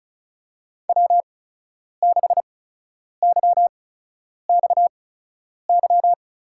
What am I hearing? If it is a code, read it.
Morse code, heard as a keyed tone, about 36 wpm, spaced as for 10 wpm, 700 Hz. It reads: W6YXY
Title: USA Call Signs - 5 Characters - 8x Character Spacing - 35wpm